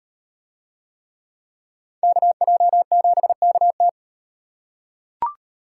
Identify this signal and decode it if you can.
Morse code, heard as a keyed tone, about 36 wpm, 700 Hz. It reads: KJ7KT